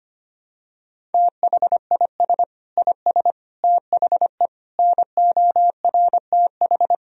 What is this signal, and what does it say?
Morse code, heard as a keyed tone, about 25 wpm, 700 Hz. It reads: THIS IS THE NORTH
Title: Sentences from Top 1000 Words - Review Part 1 - 25wpm